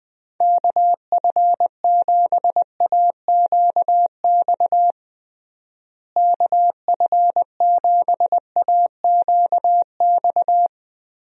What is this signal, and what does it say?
Morse code, heard as a keyed tone, about 20 wpm, 700 Hz. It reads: KF7AQX KF7AQX
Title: USA Call Signs - 6 Characters - Repeated 3x - 20wpm